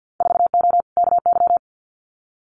Morse code, heard as a keyed tone, about 50 wpm, 700 Hz. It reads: 4O KY
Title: Winter Field Day Exchanges Encore 50wpm